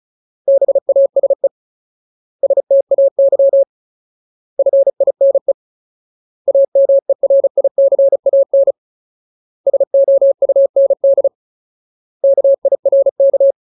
Morse code, heard as a keyed tone, about 35 wpm, 550 Hz. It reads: BASE STAY FINE AMERICAN SOUND KIRK
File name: Ultimate Word List / 35WPM Ultimate